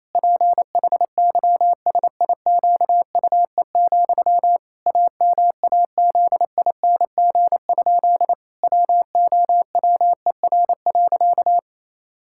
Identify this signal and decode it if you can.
Morse code, heard as a keyed tone, about 28 wpm, 700 Hz. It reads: PHYSIQUE, AMAZING? WOWER.
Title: Pangram Challenge v2 - 28wpm